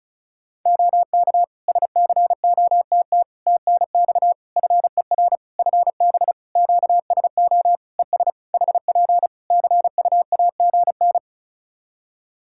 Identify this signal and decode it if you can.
Morse code, heard as a keyed tone, about 36 wpm, 700 Hz. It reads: OK SCOTT TNX FER FB QSO ES HP CUAGN
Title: Practice QSO - Exchange 4 - Ending - 35wpm